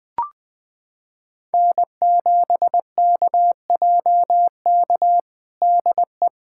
Morse code, heard as a keyed tone, about 20 wpm, 700 Hz. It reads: N7KJK DE
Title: USA Callsign DE Callsign - 5 Characters - 20wpm